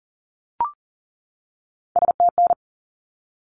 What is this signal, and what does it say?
Morse code, heard as a keyed tone, about 40 wpm, 700 Hz. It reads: STN
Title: Abbreviations 40wpm